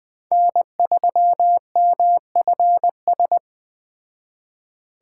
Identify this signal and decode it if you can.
Morse code, heard as a keyed tone, about 21 wpm, 700 Hz. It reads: N3MFS